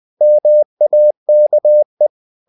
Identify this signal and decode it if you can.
Morse code, heard as a keyed tone, about 20 wpm, 600 Hz. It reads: MAKE
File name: CW / Practice / If20